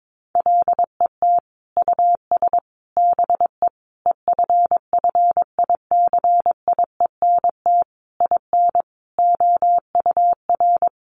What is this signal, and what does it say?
Morse code, heard as a keyed tone, about 22 wpm, 700 Hz. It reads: LET US BE EFFICIENT IN OUR
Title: Sentences from Top 1300 Words 22wpm